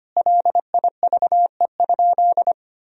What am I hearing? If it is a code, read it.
Morse code, heard as a keyed tone, about 25 wpm, 700 Hz. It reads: LIVE?
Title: Sentences from Top 1000 Words - Review Part 4 - 25wpm